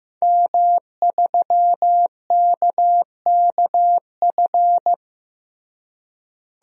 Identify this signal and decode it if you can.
Morse code, heard as a keyed tone, about 15 wpm, 700 Hz. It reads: M3KKF
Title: DXCC-100 Call Signs - 5 Characters - Repeated 3x - 15wpm